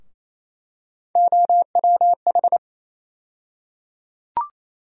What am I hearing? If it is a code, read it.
Morse code, heard as a keyed tone, about 28 wpm, 700 Hz. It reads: OWH
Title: USA Call Sign Postfixes - 3 Letters 28wpm